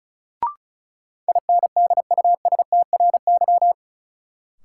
Morse code, heard as a keyed tone, about 34 wpm, 700 Hz. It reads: INDUSTRY
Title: Sentences from Top 1300 Words - New Word Drill Alt - 35wpm